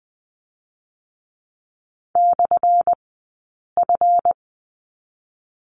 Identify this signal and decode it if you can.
Morse code, heard as a keyed tone, about 19 wpm, spaced as for 8 wpm, 700 Hz. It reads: /F
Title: 204 Lesson 33d - Letter j - Two Characters 20wpm